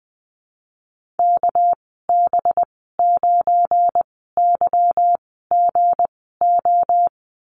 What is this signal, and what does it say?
Morse code, heard as a keyed tone, about 19 wpm, spaced as for 14 wpm, 700 Hz. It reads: KB9YGO
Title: 200 Lesson 32g - Letter k - Call Signs 20wpm